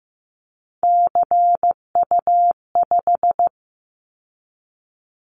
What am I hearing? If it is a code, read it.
Morse code, heard as a keyed tone, about 15 wpm, 700 Hz. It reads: CU5
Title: Callsign Prefixes for Top 50 DXCC - 15wpm